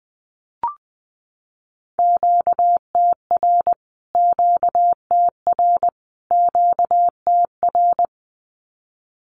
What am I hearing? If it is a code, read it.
Morse code, heard as a keyed tone, about 20 wpm, 700 Hz. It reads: QTR QTR QTR
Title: Q-Codes Repeated 3x 20wpm